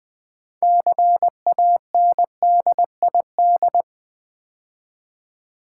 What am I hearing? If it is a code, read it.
Morse code, heard as a keyed tone, about 20 wpm, 700 Hz. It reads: CANDID
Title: Build A Word - 3 Letters At A Time - 9-Letter Words - 20wpm